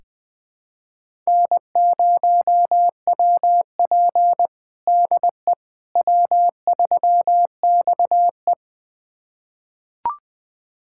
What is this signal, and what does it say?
Morse code, heard as a keyed tone, about 20 wpm, 700 Hz. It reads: N0WP DE W3XE
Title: USA Callsign DE Callsign - 4 Characters - 20wpm